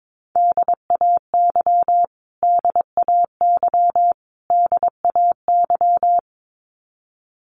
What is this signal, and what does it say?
Morse code, heard as a keyed tone, about 22 wpm, 700 Hz. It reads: DAY DAY DAY
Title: Top 200 Words Repeated 22wpm